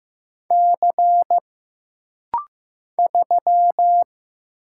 Morse code, heard as a keyed tone, about 15 wpm, 700 Hz. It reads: C 3
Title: Single Letter-Number Alt - Rapid-Fire 15wpm